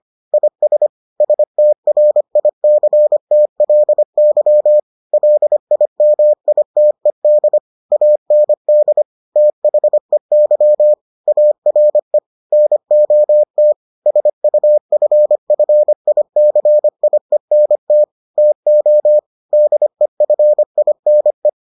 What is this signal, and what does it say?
Morse code, heard as a keyed tone, about 25 wpm, 600 Hz. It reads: IS STRICTLY LIMITED AND THEY ARE NOT SUFFICIENT TO DEFINE